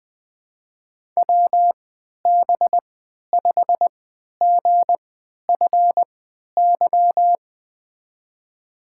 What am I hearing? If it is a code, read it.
Morse code, heard as a keyed tone, about 21 wpm, spaced as for 12 wpm, 700 Hz. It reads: WB5GFY
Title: USA Call Signs - 3x Character Spacing - 20wpm